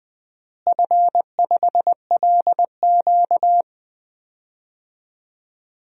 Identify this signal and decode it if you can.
Morse code, heard as a keyed tone, about 20 wpm, 700 Hz. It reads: F5LQ
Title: DXCC-100 Call Signs - 4 Characters - Repeated 3x - 20wpm